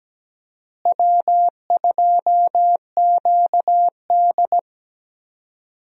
Morse code, heard as a keyed tone, about 17 wpm, 700 Hz. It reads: W2QD
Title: USA Call Signs - 4 Characters - Repeated 3x - 17wpm